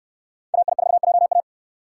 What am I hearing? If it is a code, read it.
Morse code, heard as a keyed tone, about 69 wpm, 700 Hz. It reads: AI4JA